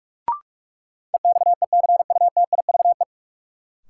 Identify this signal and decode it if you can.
Morse code, heard as a keyed tone, about 46 wpm, 700 Hz. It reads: EXECUTIVE
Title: Sets of 5 Words Encore - New Word Drill - 45wpm